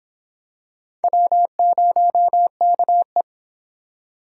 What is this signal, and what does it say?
Morse code, heard as a keyed tone, about 24 wpm, 700 Hz. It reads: W0KE